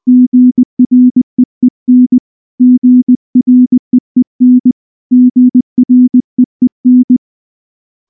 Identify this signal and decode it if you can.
Morse code, heard as a keyed tone, about 19 wpm, 250 Hz. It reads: GREEN GREEN GREEN
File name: morse / morse-training-20WPM